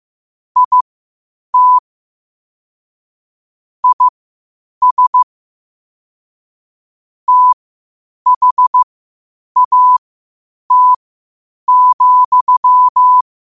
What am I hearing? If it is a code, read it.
Morse code, heard as a keyed tone, about 15 wpm, spaced as for 8 wpm, 1000 Hz. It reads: IT IS THAT,
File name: qotdfiles / 8wpm